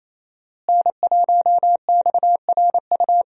Morse code, heard as a keyed tone, about 28 wpm, 700 Hz. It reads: N1XRU
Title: USA Call Signs - 5 Characters - Repeated 2x - 28wpm